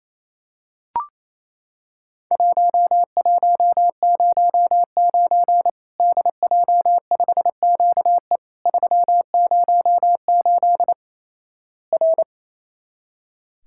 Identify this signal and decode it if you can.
Morse code, heard as a keyed tone, about 28 wpm, 700 Hz. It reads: 1109 DJ5QE 308 R
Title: Worked All Europe DX Contest - QTC Exchange - 28wpm